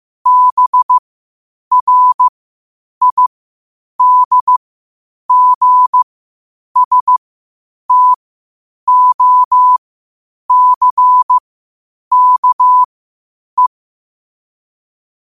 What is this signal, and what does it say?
Morse code, heard as a keyed tone, about 15 wpm, spaced as for 8 wpm, 1000 Hz. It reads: BRIDGSTOCKE
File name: qotdfiles / 8wpm